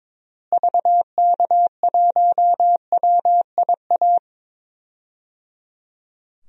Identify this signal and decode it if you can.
Morse code, heard as a keyed tone, about 22 wpm, 700 Hz. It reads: VK1WIA
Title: DXCC-100 Call Signs - 6 Characters - Repeated 3x - 22wpm